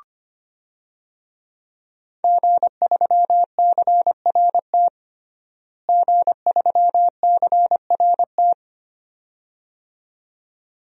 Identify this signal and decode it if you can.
Morse code, heard as a keyed tone, about 25 wpm, 700 Hz. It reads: G3CRT G3CRT
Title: DXCC-100 Call Signs - 5 Characters - Repeated 2x - 25wpm